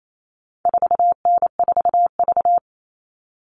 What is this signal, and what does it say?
Morse code, heard as a keyed tone, about 28 wpm, 700 Hz. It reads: 4N4V
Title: All Cut Numbers - 4-Digit 28wpm